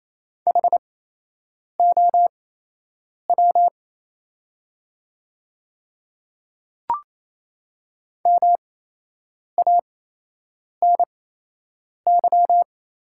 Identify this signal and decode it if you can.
Morse code, heard as a keyed tone, about 28 wpm, spaced as for 8 wpm, 690 Hz. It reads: HOW MANY
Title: 8x Character Spacing - Top 100 Words - 28wpm